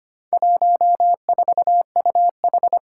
Morse code, heard as a keyed tone, about 25 wpm, 700 Hz. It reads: 14UH
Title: Four Challenging Letter-Number Combinations 25wpm